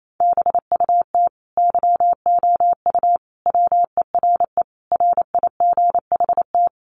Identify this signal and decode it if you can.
Morse code, heard as a keyed tone, about 28 wpm, 700 Hz. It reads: BUT YOU WERE RIGHT
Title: Sentences from Top 800 Words 28wpm